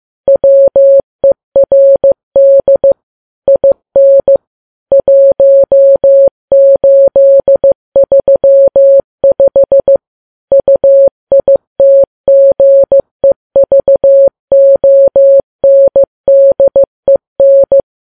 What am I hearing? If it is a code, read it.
Morse code, heard as a keyed tone, about 15 wpm, 550 Hz. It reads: WERD IN 1835 UITGEVONDEN
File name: Files Web tls / Morse on9fv / morse15kps